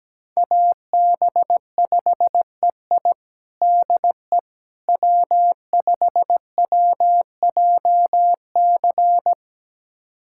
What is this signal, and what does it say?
Morse code, heard as a keyed tone, about 17 wpm, 700 Hz. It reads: AB5EI DE W5WJC
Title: USA Callsign DE Callsign - 5 Characters - 17wpm